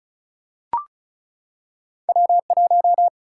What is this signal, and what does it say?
Morse code, heard as a keyed tone, about 36 wpm, 700 Hz. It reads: W 1
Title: Single-Letter Single-Number 35wpm